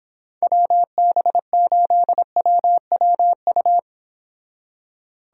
Morse code, heard as a keyed tone, about 26 wpm, 700 Hz. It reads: WB8WWU